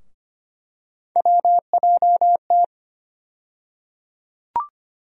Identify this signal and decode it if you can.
Morse code, heard as a keyed tone, about 25 wpm, 700 Hz. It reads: WJT
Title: USA Call Sign Postfixes - 3 Letters 25wpm